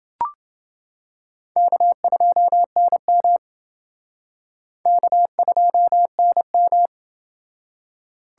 Morse code, heard as a keyed tone, about 29 wpm, 700 Hz. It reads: K2NM K2NM